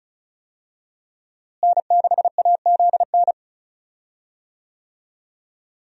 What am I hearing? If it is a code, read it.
Morse code, heard as a keyed tone, about 34 wpm, 700 Hz. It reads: N6AZN
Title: USA Call Signs - 5 Characters - Repeated 2x - 35wpm